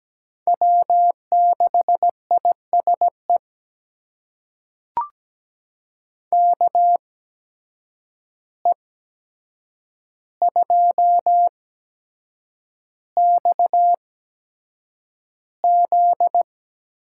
Morse code, heard as a keyed tone, about 17 wpm, spaced as for 7 wpm, 700 Hz. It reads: W6ISE KE2XZ
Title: USA Call Signs - 5 Characters - 8x Character Spacing - 17wpm